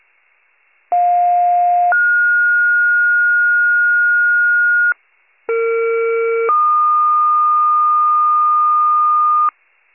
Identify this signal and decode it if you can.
Two-tone page, heard as a one-second tone, then a three-second tone, about 710 then 1430 Hz, about 460 then 1220 Hz, with noise. It is heard from 0.9 to 9.5 s.